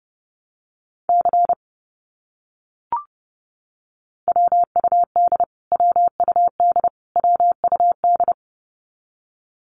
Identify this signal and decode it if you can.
Morse code, heard as a keyed tone, about 30 wpm, 700 Hz. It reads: C WUD WUD WUD
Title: Abbreviations Repeated 3x - 30wpm